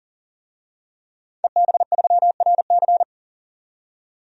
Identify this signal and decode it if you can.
Morse code, heard as a keyed tone, about 37 wpm, 700 Hz. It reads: EB3RC
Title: DXCC-100 Call Signs - 5 Characters - 40wpm